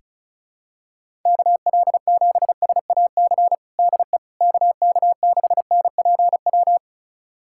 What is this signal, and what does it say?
Morse code, heard as a keyed tone, about 34 wpm, 700 Hz. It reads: KL7SAC DE KK6NPW
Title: USA Callsign DE Callsign - 6 Characters - 35wpm